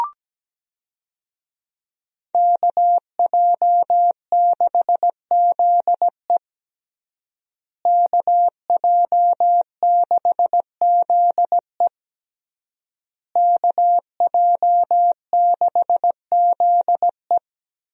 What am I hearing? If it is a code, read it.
Morse code, heard as a keyed tone, about 17 wpm, 700 Hz. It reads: KJ6ZE KJ6ZE KJ6ZE
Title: USA Call Signs - 5 Characters - Repeated 3x - 17wpm